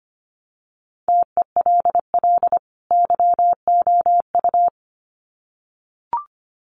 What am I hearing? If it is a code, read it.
Morse code, heard as a keyed tone, about 25 wpm, 700 Hz. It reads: TELL YOU
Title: Sets of 2 Words 25wpm